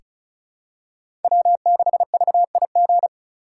Morse code, heard as a keyed tone, about 36 wpm, 700 Hz. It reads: W6VIG
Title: USA Call Signs - 5 Characters - 4x Character Spacing - 35wpm